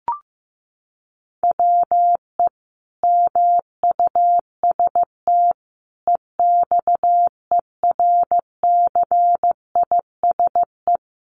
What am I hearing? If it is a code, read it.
Morse code, heard as a keyed tone, about 15 wpm, 700 Hz. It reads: WE MUST EXERCISE
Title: Sentences from Top 1200 Words 15wpm